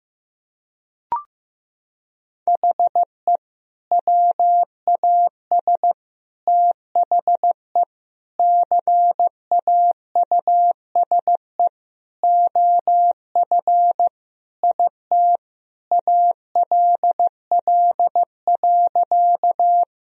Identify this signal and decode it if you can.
Morse code, heard as a keyed tone, about 15 wpm, 700 Hz. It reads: HE WAS THE CAUSE OF IT ALL.
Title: Sentences from Top 1000 Words - Review Part 1 - 15wpm